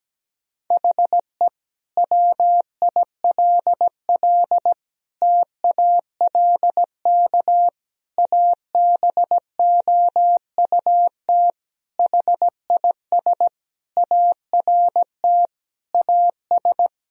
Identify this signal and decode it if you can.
Morse code, heard as a keyed tone, about 17 wpm, 700 Hz. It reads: HE WILL TALK ABOUT HIS ART AS
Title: Sentences from Top 1000 Words - Review Part 3 - 17wpm